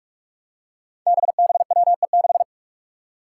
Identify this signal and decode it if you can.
Morse code, heard as a keyed tone, about 46 wpm, 700 Hz. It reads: BBWE6
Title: Five Challenging Letter-Number Combinations 45wpm